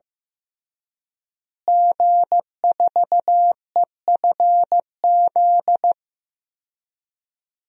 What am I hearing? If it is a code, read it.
Morse code, heard as a keyed tone, about 15 wpm, 700 Hz. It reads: G4EFZ
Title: DXCC-100 Call Signs - 5 Characters - Repeated 2x - 15wpm